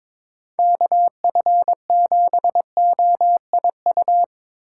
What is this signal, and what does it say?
Morse code, heard as a keyed tone, about 22 wpm, 700 Hz. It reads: KF7OIU
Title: USA Call Signs - 6 Characters - 3x Character Spacing - 22wpm